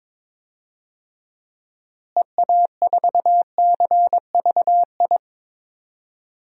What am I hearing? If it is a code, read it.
Morse code, heard as a keyed tone, about 22 wpm, 700 Hz. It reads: EA4CVI
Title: DXCC-100 Call Signs - 6 Characters - Repeated 2x - 22wpm